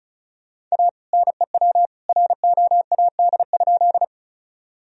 Morse code, heard as a keyed tone, about 34 wpm, 700 Hz. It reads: A NEW ROAD?
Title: Sentences from Top 1000 Words - Review Part 2 - 35wpm